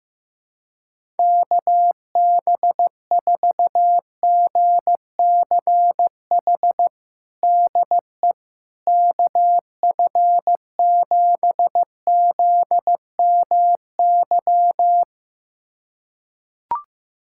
Morse code, heard as a keyed tone, about 15 wpm, 700 Hz. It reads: KB4GCH DE KF7ZMY
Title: USA Callsign DE Callsign 15wpm